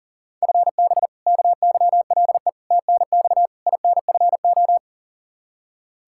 Morse code, heard as a keyed tone, about 40 wpm, 700 Hz. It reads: FB KYLE TNX INFO
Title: Practice QSO - Exchange 3 - More Info - 40wpm